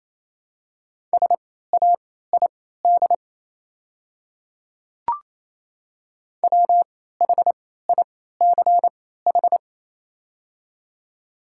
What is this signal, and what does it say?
Morse code, heard as a keyed tone, about 28 wpm, spaced as for 16 wpm, 700 Hz. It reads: SAID WHICH
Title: 3x Character Spacing - Top 100 Words - 28wpm